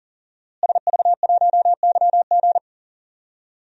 Morse code, heard as a keyed tone, about 40 wpm, 690 Hz. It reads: SV1YG
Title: DXCC-100 Call Signs - 5 Characters - Repeated 2x - 40wpm